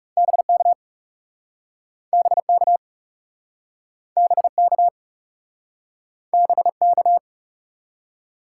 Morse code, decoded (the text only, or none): BK BK BK BK